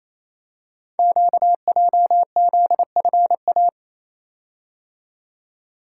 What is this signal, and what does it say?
Morse code, heard as a keyed tone, about 28 wpm, 700 Hz. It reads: QJZFA
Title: Code Groups - 5 Letters - 28wpm